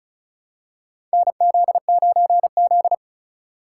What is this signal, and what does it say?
Morse code, heard as a keyed tone, about 36 wpm, 700 Hz. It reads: NZ9Z
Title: USA Call Signs - 4 Characters - 3x Character Spacing - 35wpm